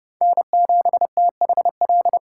Morse code, heard as a keyed tone, about 28 wpm, 700 Hz. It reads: N7THL